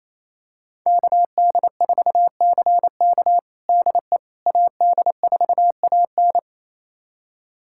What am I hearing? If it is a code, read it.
Morse code, heard as a keyed tone, about 28 wpm, 700 Hz. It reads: KD4CK DE AD4AN
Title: USA Callsign DE Callsign - 5 Characters - 28wpm